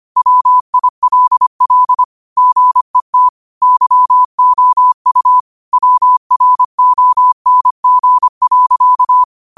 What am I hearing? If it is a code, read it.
Morse code, heard as a keyed tone, about 25 wpm, 1000 Hz. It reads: WILL GET YOU WRONG.